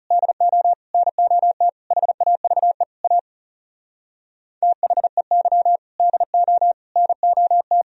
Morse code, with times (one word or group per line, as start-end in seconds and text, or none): 0.1-0.7 DO
0.9-1.7 NOT
1.9-2.8 HAVE
3.0-3.2 A
4.6-5.8 THEY
6.0-6.7 DO
7.0-7.8 NOT